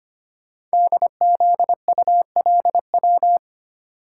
Morse code, heard as a keyed tone, about 25 wpm, 700 Hz. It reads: DZULW
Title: Code Groups - 5 Letters - 25wpm